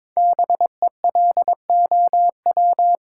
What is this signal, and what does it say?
Morse code, heard as a keyed tone, about 22 wpm, 700 Hz. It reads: BELOW
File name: 5 Letter Words - First 3 Letters Given - 22wpm